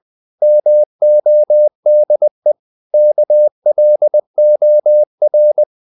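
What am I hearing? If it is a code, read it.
Morse code, heard as a keyed tone, about 20 wpm, 600 Hz. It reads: MODE KLOR
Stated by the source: Morse code practice of short Danish words at 20 wpm, from CW med Gnister.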